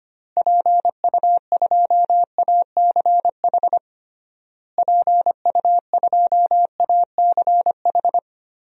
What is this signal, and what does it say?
Morse code, heard as a keyed tone, about 25 wpm, 700 Hz. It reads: PU2ACH PU2ACH
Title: DXCC-100 Call Signs - 6 Characters - Repeated 3x - 25wpm